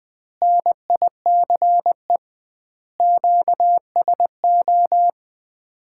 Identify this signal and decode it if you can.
Morse code, heard as a keyed tone, about 21 wpm, 700 Hz. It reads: NICE QSO
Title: Common QSO Phrases - 2x Word-Spacing - 20wpm